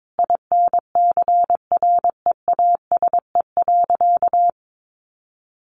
Morse code, heard as a keyed tone, about 22 wpm, 700 Hz. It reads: INCREASE.
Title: Sentences from Top 1400 Words 22wpm